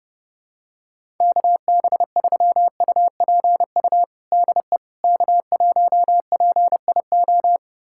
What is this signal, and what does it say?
Morse code, heard as a keyed tone, about 30 wpm, 700 Hz. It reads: KB3UPU DE K1PIO
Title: USA Callsign DE Callsign 30wpm